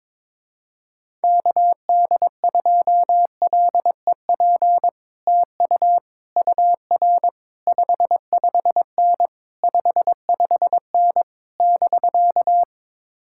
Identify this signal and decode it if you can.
Morse code, heard as a keyed tone, about 21 wpm, 700 Hz. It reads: KD2LEP TU UR 55N 55N <BK>
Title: Speed-Racing Common QSO Phrases - 22wpm